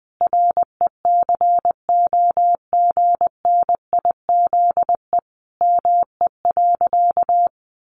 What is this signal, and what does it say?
Morse code, heard as a keyed tone, about 20 wpm, 700 Hz. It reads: RECOGNIZE ME.
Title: Sentences from Top 1400 Words 20wpm